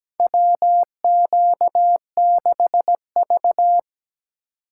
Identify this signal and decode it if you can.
Morse code, heard as a keyed tone, about 17 wpm, 700 Hz. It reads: WQ6V
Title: USA Call Signs - 4 Characters - 4x Character Spacing - 17wpm